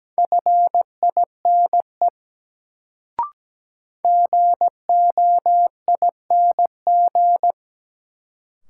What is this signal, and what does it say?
Morse code, heard as a keyed tone, about 17 wpm, 700 Hz. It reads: FINE GOING
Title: Sentences from Top 500 Words - New Word Drill - 17wpm